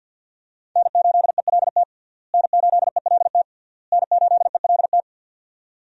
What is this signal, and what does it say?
Morse code, heard as a keyed tone, about 50 wpm, 700 Hz. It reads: N8ELT N8ELT N8ELT
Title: USA Call Signs - 5 Characters - Repeated 3x - 50wpm